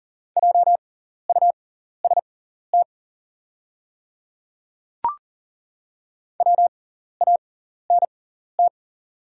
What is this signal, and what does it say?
Morse code, heard as a keyed tone, about 40 wpm, spaced as for 14 wpm, 700 Hz. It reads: JUST WANT
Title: 6x Character Spacing - Top 200 Words - 40wpm